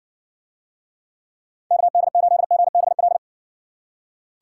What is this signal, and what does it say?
Morse code, heard as a keyed tone, about 59 wpm, 700 Hz. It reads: KD8GBL